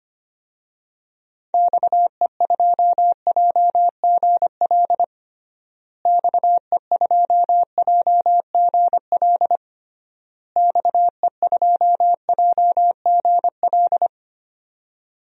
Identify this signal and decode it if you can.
Morse code, heard as a keyed tone, about 25 wpm, 700 Hz. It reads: XE2JGL XE2JGL XE2JGL
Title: DXCC-100 Call Signs - 6 Characters - Repeated 3x - 25wpm